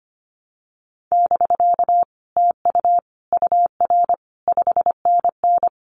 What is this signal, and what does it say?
Morse code, heard as a keyed tone, about 25 wpm, 700 Hz. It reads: <BK> TU UR 5NN
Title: Common QSO Phrases Encore 25wpm